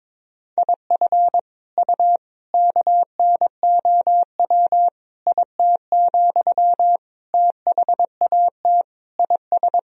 Morse code, heard as a keyed tone, about 22 wpm, 700 Hz. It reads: IF U KNOW IT, THAT IS
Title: Sentences from Top 400 Words and Abbreviations 22wpm